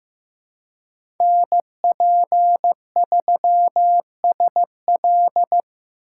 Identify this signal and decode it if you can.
Morse code, heard as a keyed tone, about 16 wpm, 700 Hz. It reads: NP3SL
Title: USA Call Signs - 5 Characters - 2x Character Spacing - 15wpm